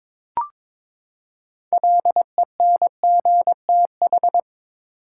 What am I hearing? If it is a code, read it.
Morse code, heard as a keyed tone, about 22 wpm, 700 Hz. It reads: LENGTH
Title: Six Letter Words - 22wpm